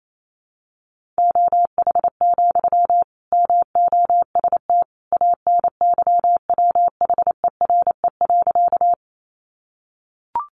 Morse code, heard as a keyed tone, about 28 wpm, 700 Hz. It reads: OH, MOST ANYWHERE.
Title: Sentences from Top 1400 Words 28wpm